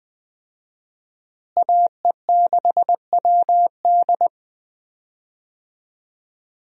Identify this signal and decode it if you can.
Morse code, heard as a keyed tone, about 20 wpm, 700 Hz. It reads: AE6WD
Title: USA Call Signs - 5 Characters - Repeated 3x - 20wpm